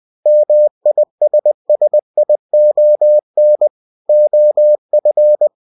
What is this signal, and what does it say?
Morse code, heard as a keyed tone, about 20 wpm, 600 Hz. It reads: MISSION OF
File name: cw-books / winnie-the-pooh / winnie-20wpm-page-0114